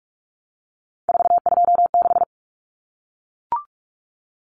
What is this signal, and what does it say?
Morse code, heard as a keyed tone, about 44 wpm, 700 Hz. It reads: 426
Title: Common Cut Numbers - 3-Digit 45wpm